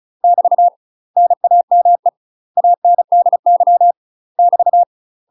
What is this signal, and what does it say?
Morse code, heard as a keyed tone, about 34 wpm, 700 Hz. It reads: <BT> NAME ANDY <BT>
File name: UK MASTER QSOs TRAINING / 35WPM / UK MASTER QSOs 0039